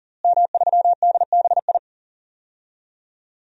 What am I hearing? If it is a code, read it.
Morse code, heard as a keyed tone, about 40 wpm, 700 Hz. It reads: M3DBI
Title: DXCC-100 Call Signs - 5 Characters - Repeated 2x - 40wpm